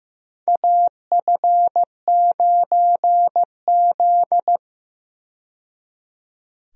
Morse code, heard as a keyed tone, about 15 wpm, 700 Hz. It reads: AF9Z
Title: USA Call Signs - 4 Characters - 15wpm